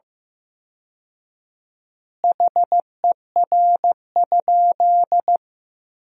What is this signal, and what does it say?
Morse code, heard as a keyed tone, about 15 wpm, 700 Hz. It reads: HER?
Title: 4x Word-Spacing - Sentences from Top 400 Words - 15wpm